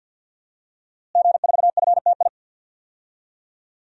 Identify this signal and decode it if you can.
Morse code, heard as a keyed tone, about 52 wpm, 700 Hz. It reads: G4FTI
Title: DXCC-100 Call Signs - 5 Characters - Repeated 3x - 50wpm